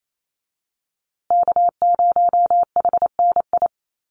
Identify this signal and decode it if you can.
Morse code, heard as a keyed tone, about 28 wpm, 700 Hz. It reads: K0HNI
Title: USA Call Signs Repeated 3x 28wpm